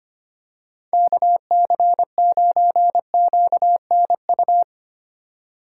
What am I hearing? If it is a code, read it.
Morse code, heard as a keyed tone, about 25 wpm, 700 Hz. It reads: KC9QNU
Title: USA Call Signs - 6 Characters - 2x Character Spacing - 25wpm